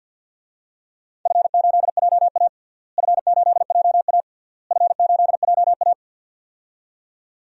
Morse code, heard as a keyed tone, about 51 wpm, 700 Hz. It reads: F8JA F8JA F8JA
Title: DXCC-100 Call Signs - 4 Characters - Repeated 3x - 50wpm